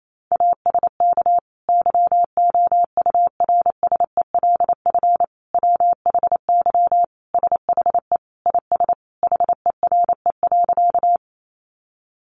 Morse code, heard as a keyed tone, about 28 wpm, 700 Hz. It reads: ASK YOURSELF WHY SHE IS HERE.
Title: Sentences from Top 600 Words 28wpm